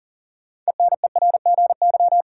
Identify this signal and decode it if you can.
Morse code, heard as a keyed tone, about 39 wpm, 700 Hz. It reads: ENERGY
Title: Sentences from Top 900 Words - New Word Drill - 40wpm